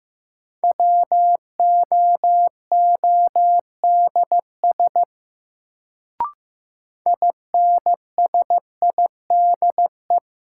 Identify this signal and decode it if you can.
Morse code, heard as a keyed tone, about 15 wpm, 700 Hz. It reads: WOODS INSIDE
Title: Sentences from Top 1000 Words - Review Part 2 - New Word Drill - 15wpm